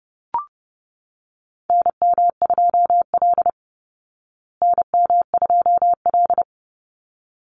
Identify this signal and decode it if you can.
Morse code, heard as a keyed tone, about 30 wpm, 700 Hz. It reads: NM2L NM2L
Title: Call Signs Repeated 3x - 30wpm